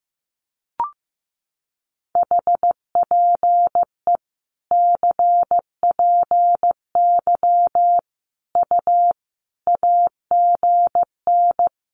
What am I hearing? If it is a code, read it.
Morse code, heard as a keyed tone, about 14 wpm, 700 Hz. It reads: HPE CPY U AGN
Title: Common QSO Phrases 15wpm